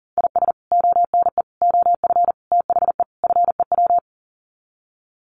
Morse code, heard as a keyed tone, about 40 wpm, 700 Hz. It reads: IS ONE OF THE FEW
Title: Sets of 5 Words 40wpm